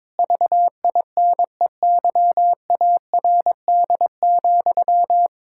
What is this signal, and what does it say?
Morse code, heard as a keyed tone, about 22 wpm, 700 Hz. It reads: VINEYARD,
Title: Pangram Challenge v2 - 22wpm